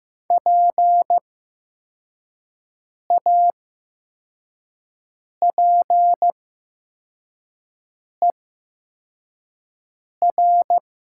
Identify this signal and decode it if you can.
Morse code, heard as a keyed tone, about 15 wpm, spaced as for 4 wpm, 700 Hz. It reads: PAPER
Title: 8x Character Spacing - Top 300 Words - 15wpm